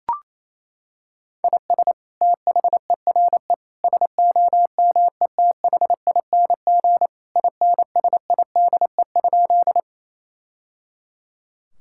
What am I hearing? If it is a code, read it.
Morse code, heard as a keyed tone, about 28 wpm, 700 Hz. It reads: IS THERE SOMETHING INSIDE?
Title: Sentences from Top 1000 Words - Review Part 6 - 28wpm